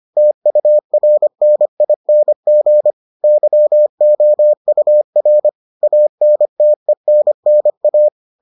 Morse code, heard as a keyed tone, about 25 wpm, 600 Hz. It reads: TURNING YOUR ANTENNA